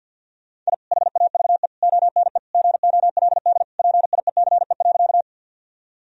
Morse code, heard as a keyed tone, about 49 wpm, 700 Hz. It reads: I HAVE ONE GOLD PIECE.
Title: Sentences from Top 1000 Words - Review Part 3 - 50wpm